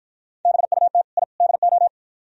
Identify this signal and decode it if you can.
Morse code, heard as a keyed tone, about 52 wpm, 690 Hz. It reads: BUT I DO